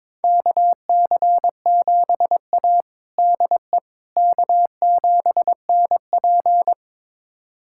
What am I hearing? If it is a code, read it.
Morse code, heard as a keyed tone, about 21 wpm, 700 Hz. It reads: KC7A DE K7NP
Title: USA Callsign DE Callsign - 4 Characters - 22wpm